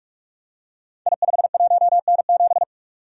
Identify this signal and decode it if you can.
Morse code, heard as a keyed tone, about 44 wpm, 700 Hz. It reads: I51N7
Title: Code Groups - 5 Letter:Numbers - 45wpm